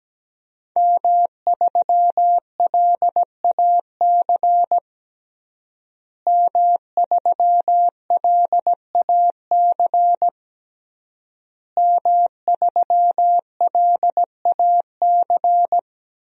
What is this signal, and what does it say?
Morse code, heard as a keyed tone, about 17 wpm, 700 Hz. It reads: M3LAC M3LAC M3LAC
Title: DXCC-100 Call Signs - 5 Characters - Repeated 3x - 17wpm